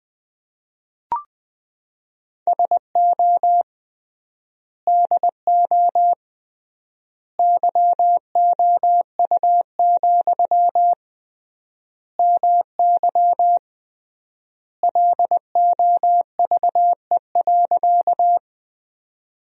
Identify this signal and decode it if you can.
Morse code, heard as a keyed tone, about 20 wpm, 700 Hz. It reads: SO DO YOU, MY LOVE.
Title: 3x Word-Spacing - Sentences from Top 400 Words - 20wpm